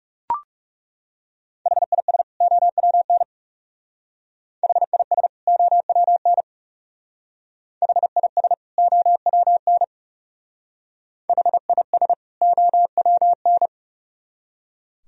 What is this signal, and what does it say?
Morse code, heard as a keyed tone, about 39 wpm, 700 Hz. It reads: HIS OWN HIS OWN HIS OWN HIS OWN